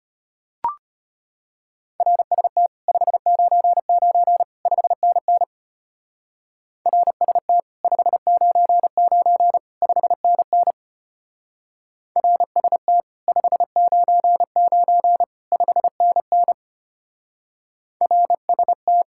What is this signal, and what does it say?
Morse code, heard as a keyed tone, about 31 wpm, 700 Hz. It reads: RST 599 5NN RST 599 5NN RST 599 5NN RST
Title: Speed-Racing Common QSO Phrases - 25wpm